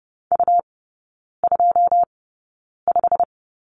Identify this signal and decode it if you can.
Morse code, heard as a keyed tone, about 30 wpm, spaced as for 9 wpm, 690 Hz. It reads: U25
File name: Lesson 17e - Letter u - Three Characters 30wpm